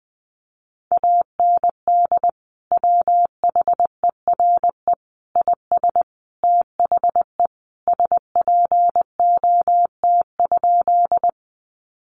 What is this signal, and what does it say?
Morse code, heard as a keyed tone, about 19 wpm, 700 Hz. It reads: AND WHERE IS THE SPOT?
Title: Sentences from Top 1100 Words 20wpm